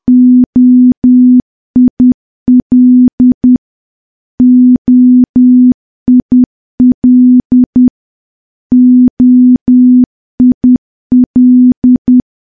Morse code, heard as a keyed tone, about 10 wpm, 250 Hz. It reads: OIL OIL OIL
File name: morse / morse-training-10WPM